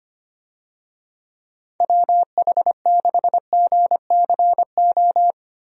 Morse code, heard as a keyed tone, about 25 wpm, 700 Hz. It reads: WH6GCO
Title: USA Call Signs - 6 Characters - Repeated 3x - 25wpm